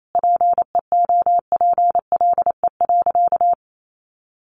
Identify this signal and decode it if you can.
Morse code, heard as a keyed tone, about 28 wpm, 700 Hz. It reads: PEOPLE.
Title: Sentences from Top 1500 Words 28wpm